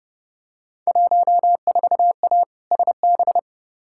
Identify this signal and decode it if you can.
Morse code, heard as a keyed tone, about 32 wpm, 700 Hz. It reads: 14A SB
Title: Field Day Exchanges Encore - 30wpm